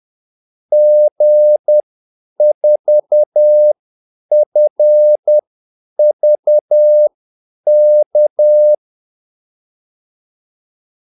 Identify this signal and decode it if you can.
Morse code, heard as a keyed tone, about 10 wpm, spaced as for 8 wpm, 600 Hz. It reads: G4FVK
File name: Callsigns-10-8wpm